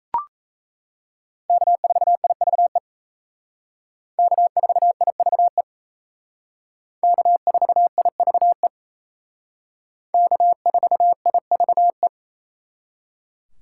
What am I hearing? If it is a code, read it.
Morse code, heard as a keyed tone, about 36 wpm, 700 Hz. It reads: K4IVE K4IVE K4IVE K4IVE